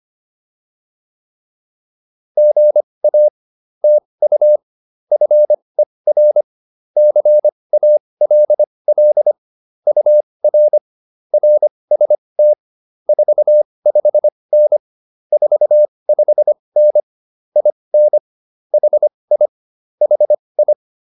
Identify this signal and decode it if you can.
Morse code, heard as a keyed tone, about 25 wpm, spaced as for 20 wpm, 600 Hz. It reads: GA TU FER CALL UR RST 45N 45N IN HI HI